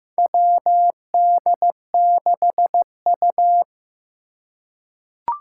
Morse code, heard as a keyed tone, about 15 wpm, 700 Hz. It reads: WD6U
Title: USA Call Signs - 4 Characters - 4x Character Spacing - 15wpm